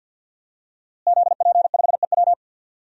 Morse code, heard as a keyed tone, about 50 wpm, 700 Hz. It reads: ZP5EW
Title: DXCC-100 Call Signs - 5 Characters - Repeated 2x - 50wpm